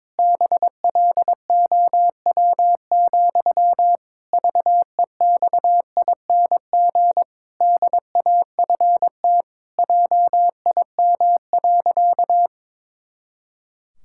Morse code, heard as a keyed tone, about 21 wpm, 700 Hz. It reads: BLOW, VEXING DAFT JIM.
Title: Pangram Challenge v2 - 22wpm